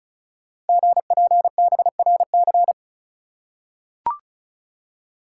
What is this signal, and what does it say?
Morse code, heard as a keyed tone, about 36 wpm, 690 Hz. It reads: GPBRC
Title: Code Groups - 5 Letters - 35wpm